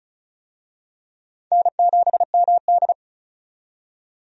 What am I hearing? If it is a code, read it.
Morse code, heard as a keyed tone, about 36 wpm, 700 Hz. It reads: N7MD
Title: USA Call Signs - 4 Characters - 35wpm